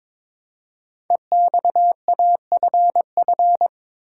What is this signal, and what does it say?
Morse code, heard as a keyed tone, about 22 wpm, 700 Hz. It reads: EXAFF
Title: Code Groups - 5 Letters - 22wpm